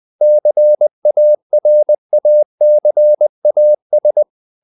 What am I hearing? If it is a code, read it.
Morse code, heard as a keyed tone, about 20 wpm, 600 Hz. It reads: CARACAS
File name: CW / Practice / NationalCapitals20